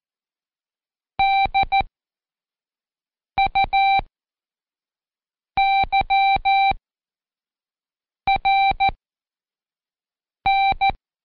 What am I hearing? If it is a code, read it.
Morse code, heard as a keyed tone, about 14 wpm, 780 Hz. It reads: D U Y R N
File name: Code Quick / AudioDownld1 / Seg40-Day5-CheckTest 5